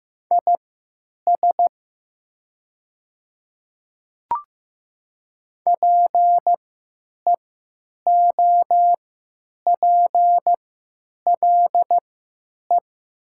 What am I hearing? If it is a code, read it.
Morse code, heard as a keyed tone, about 15 wpm, spaced as for 9 wpm, 700 Hz. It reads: IS PEOPLE
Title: 3x Character Spacing - Top 100 Words - 15wpm